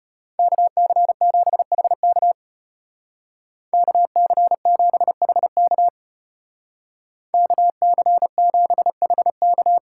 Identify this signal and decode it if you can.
Morse code, heard as a keyed tone, about 31 wpm, 700 Hz. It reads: KC7HK KC7HK KC7HK
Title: Speed Racing - USA 2x2 Call Signs 25wpm